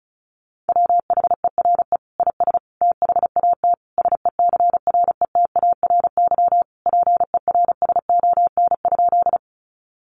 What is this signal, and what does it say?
Morse code, heard as a keyed tone, about 34 wpm, 700 Hz. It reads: WHERE IS THAT SECRETARY PERSON?
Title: Sentences from Top 1300 Words 35wpm